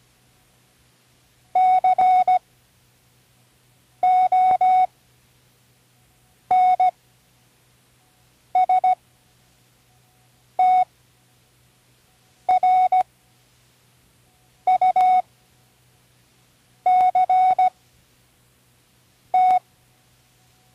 Morse code, heard as a keyed tone, about 17 wpm, spaced as for 5 wpm, 740 Hz. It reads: CONSTRUCT
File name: Learning Morse Code (from Tune in the World)